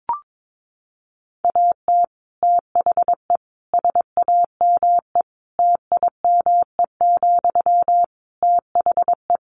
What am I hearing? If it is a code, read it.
Morse code, heard as a keyed tone, about 22 wpm, 700 Hz. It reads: AT THE SAME TIME, THE
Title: Sentences from Top 1500 Words 22wpm